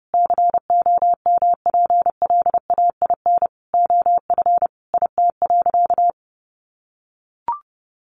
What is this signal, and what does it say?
Morse code, heard as a keyed tone, about 30 wpm, 700 Hz. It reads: COMPLAIN OF IT.
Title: Sentences from Top 1300 Words 30wpm